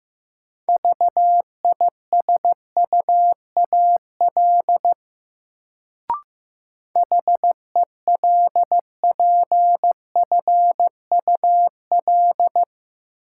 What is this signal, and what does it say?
Morse code, heard as a keyed tone, about 15 wpm, 700 Hz. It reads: VISUAL HELPFUL
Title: Sentences from Top 1000 Words - Review Part 6 - New Word Drill - 15wpm